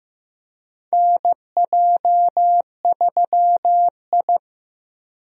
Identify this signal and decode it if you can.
Morse code, heard as a keyed tone, about 15 wpm, 700 Hz. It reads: NJ3I
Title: USA Call Signs - 4 Characters - Repeated 2x - 15wpm